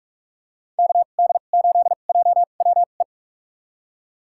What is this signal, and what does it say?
Morse code, heard as a keyed tone, about 46 wpm, spaced as for 32 wpm, 700 Hz. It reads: KD8JWE
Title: USA Call Signs - 2x Character Spacing - 45wpm